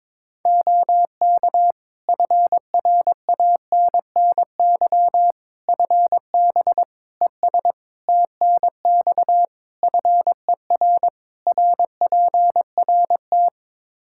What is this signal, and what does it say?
Morse code, heard as a keyed tone, about 21 wpm, 700 Hz. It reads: OK FRANNY FB ES TNX FER RPRT
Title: Practice QSO - Exchange 2 - Info - 22wpm